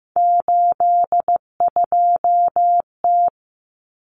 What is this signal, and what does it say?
Morse code, heard as a keyed tone, about 15 wpm, 700 Hz. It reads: 82T
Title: Common Cut Numbers - 3-Digit 15wpm